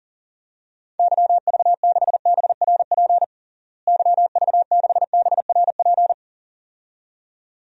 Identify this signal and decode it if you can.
Morse code, heard as a keyed tone, about 40 wpm, 700 Hz. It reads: YV6BRP YV6BRP
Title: DXCC-100 Call Signs - 6 Characters - Repeated 2x - 40wpm